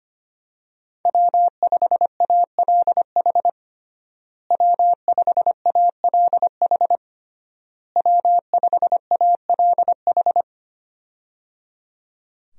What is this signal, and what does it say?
Morse code, heard as a keyed tone, about 25 wpm, 700 Hz. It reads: W5ALH W5ALH W5ALH
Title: USA Call Signs - 5 Characters - Repeated 3x - 25wpm